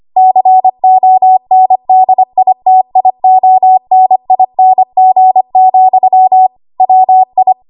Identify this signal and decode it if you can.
Morse code, heard as a keyed tone, about 25 wpm, 750 Hz. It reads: CONDITIONING, WI
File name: Morse / 25WPM